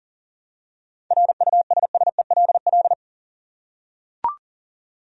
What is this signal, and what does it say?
Morse code, heard as a keyed tone, about 40 wpm, 700 Hz. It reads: RUSSELL
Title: USA Common Names - 40wpm